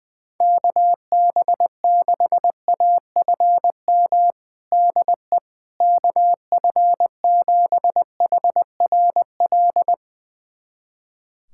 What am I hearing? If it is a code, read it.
Morse code, heard as a keyed tone, about 20 wpm, 700 Hz. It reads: KB6AFM DE KF7HRL
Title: USA Callsign DE Callsign - 6 Characters - 20wpm